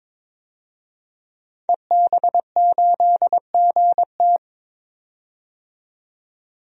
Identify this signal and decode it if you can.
Morse code, heard as a keyed tone, about 22 wpm, 700 Hz. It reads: EB8GT
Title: DXCC-100 Call Signs - 5 Characters - 22wpm